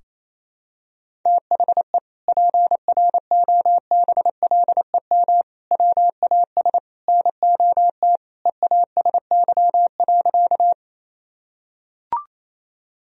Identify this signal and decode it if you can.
Morse code, heard as a keyed tone, about 28 wpm, 700 Hz. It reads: THE PROBLEM WAS NOT EASY.
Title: Sentences from Top 1000 Words - Review Part 2 - 28wpm